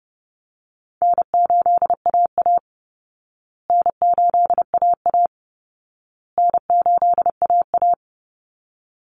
Morse code, heard as a keyed tone, about 30 wpm, 700 Hz. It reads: N8AA N8AA N8AA
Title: Call Signs Repeated 2x - 30wpm